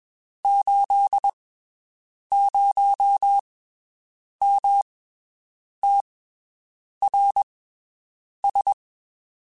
Morse code, heard as a keyed tone, about 22 wpm, spaced as for 7 wpm, 800 Hz. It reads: 80MTRS
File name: (spaced) / QSOs Spaced-mp3s / WD0FBT 1 @20 sp